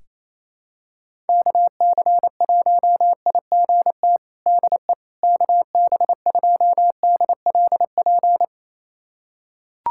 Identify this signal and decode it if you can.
Morse code, heard as a keyed tone, about 28 wpm, 700 Hz. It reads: KC1IGT DE KB2DLP
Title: USA Callsign DE Callsign - 6 Characters - 28wpm